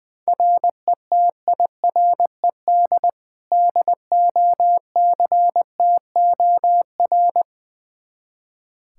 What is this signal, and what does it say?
Morse code, heard as a keyed tone, about 20 wpm, 700 Hz. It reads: RETIRED DOCTOR
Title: Practice QSO - Exchange 3 - More Info - 20wpm